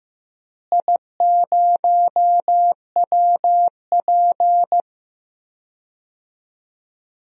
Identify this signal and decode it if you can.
Morse code, heard as a keyed tone, about 15 wpm, 700 Hz. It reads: I0WP
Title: DXCC-100 Call Signs - 4 Characters - Repeated 3x - 15wpm